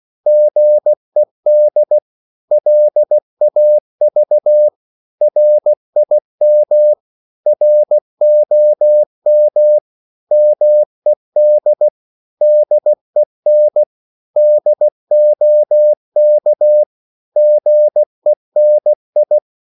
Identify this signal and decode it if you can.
Morse code, heard as a keyed tone, about 16 wpm, 600 Hz. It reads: GED LAV RIM ROM MED DEN DOK GENI
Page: Korte danske ord 16wpm